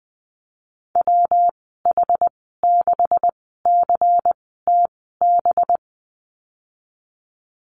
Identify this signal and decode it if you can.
Morse code, heard as a keyed tone, about 20 wpm, spaced as for 15 wpm, 700 Hz. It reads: WH6CTB
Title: 186 Lesson 30g - Letter b - Call Signs 20wpm